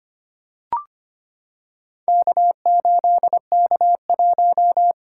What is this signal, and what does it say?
Morse code, heard as a keyed tone, about 25 wpm, 700 Hz. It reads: K8K1
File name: practice / 25wpm / Four Letter-Number Combinations - 25wpm